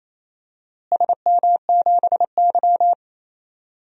Morse code, heard as a keyed tone, about 28 wpm, 700 Hz. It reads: SM7Y
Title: DXCC-100 Call Signs - 4 Characters - Repeated 2x - 28wpm